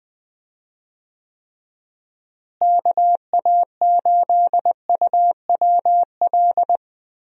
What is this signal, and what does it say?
Morse code, heard as a keyed tone, about 21 wpm, 700 Hz. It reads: KA8UWL
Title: USA Call Signs - 6 Characters - Repeated 3x - 20wpm